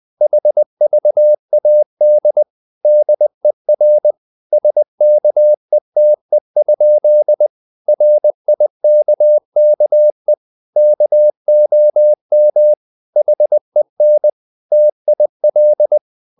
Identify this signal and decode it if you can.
Morse code, heard as a keyed tone, about 20 wpm, 600 Hz. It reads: HVAD DER SKETE? RIKKE KOM HEN TIL
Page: Ferie 20 wpm